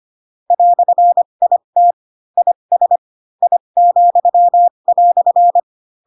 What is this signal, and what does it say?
Morse code, heard as a keyed tone, about 25 wpm, spaced as for 22 wpm, 700 Hz. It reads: "IT IS I,"
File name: Pinocchio / Pinocchio ch6 18 25 ik1hge